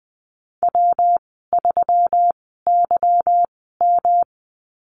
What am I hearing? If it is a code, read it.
Morse code, heard as a keyed tone, about 20 wpm, spaced as for 15 wpm, 700 Hz. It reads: W3YM
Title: 213 Lesson 34g - Number 8 - Call Signs 20wpm